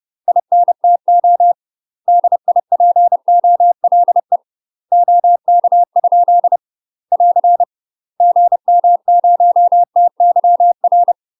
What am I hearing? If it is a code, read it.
Morse code, heard as a keyed tone, about 30 wpm, 700 Hz. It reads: INTO DIPOLE OK? <AR> GM0TYR
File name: UK MASTER QSOs TRAINING / 30WPM / UK MASTER QSOs 0042